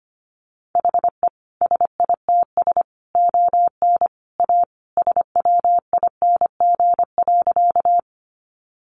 Morse code, heard as a keyed tone, about 25 wpm, 700 Hz. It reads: HE SITS ON A SWING.
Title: Additional Easy Phrases 25wpm